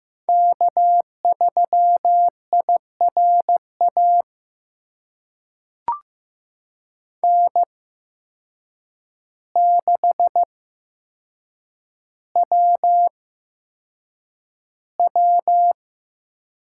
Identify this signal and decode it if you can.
Morse code, heard as a keyed tone, about 15 wpm, spaced as for 7 wpm, 700 Hz. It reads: K3IRA N6WW
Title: USA Call Signs - 5 Characters - 8x Character Spacing - 15wpm